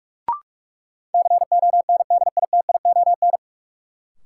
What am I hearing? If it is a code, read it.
Morse code, heard as a keyed tone, about 41 wpm, 700 Hz. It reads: CONDITION
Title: Sentences from Top 1000 Words - New Word Drill - 45wpm